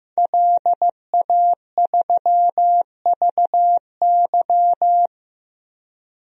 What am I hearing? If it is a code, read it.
Morse code, heard as a keyed tone, about 16 wpm, 700 Hz. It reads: LA3VY
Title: DXCC-100 Call Signs - 5 Characters - 15wpm